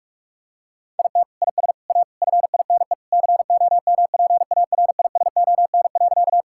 Morse code, heard as a keyed tone, about 46 wpm, 700 Hz. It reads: IT IS A FINE COMPARISON.
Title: Sentences from Top 1600 Words 45wpm